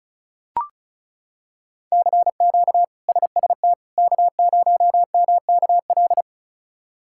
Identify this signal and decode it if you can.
Morse code, heard as a keyed tone, about 36 wpm, 700 Hz. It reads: CQ SST K0MKL
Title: K1USN SST QSO Practice - 35wpm